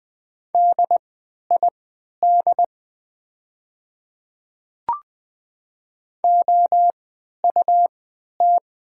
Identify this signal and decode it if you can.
Morse code, heard as a keyed tone, about 20 wpm, spaced as for 11 wpm, 700 Hz. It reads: DID OUT
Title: 3x Character Spacing - Top 100 Words - 20wpm